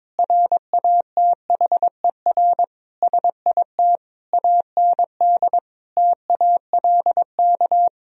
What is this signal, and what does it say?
Morse code, heard as a keyed tone, about 22 wpm, 700 Hz. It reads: RATHER SIT AND TALK
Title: Sentences from Top 1000 Words - Review Part 4 - 22wpm